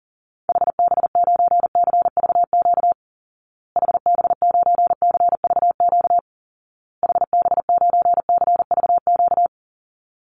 Morse code, heard as a keyed tone, about 40 wpm, 700 Hz. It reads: HB9CVQ HB9CVQ HB9CVQ
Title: Call Signs Repeated 2x - 40wpm